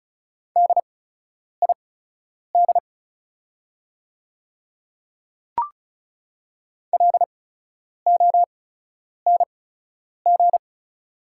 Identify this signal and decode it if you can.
Morse code, heard as a keyed tone, about 36 wpm, spaced as for 10 wpm, 700 Hz. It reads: DID LONG
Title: 8x Character Spacing - Top 100 Words - 35wpm